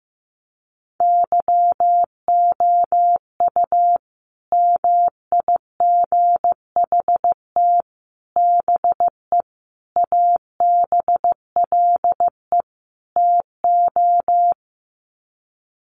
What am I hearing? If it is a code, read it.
Morse code, heard as a keyed tone, about 15 wpm, 700 Hz. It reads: YOU MIGHT BE ABLE TO
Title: Sets of 5 Words Encore2 15wpm